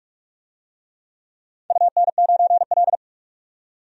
Morse code, heard as a keyed tone, about 46 wpm, 700 Hz. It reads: UN9L
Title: DXCC-100 Call Signs - 4 Characters - 45wpm